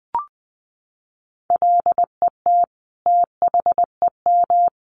Morse code, heard as a keyed tone, about 20 wpm, 700 Hz. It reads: LET THEM
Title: Sentences from Top 1500 Words 20wpm